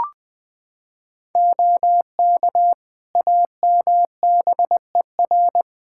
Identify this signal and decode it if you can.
Morse code, heard as a keyed tone, about 21 wpm, 700 Hz. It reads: OK AMBER
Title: Practice QSO - Exchange 3 - More Info - 20wpm